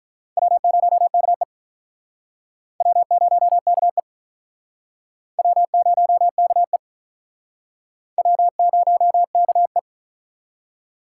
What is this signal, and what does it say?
Morse code, heard as a keyed tone, about 47 wpm, 700 Hz. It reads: W0KE W0KE W0KE W0KE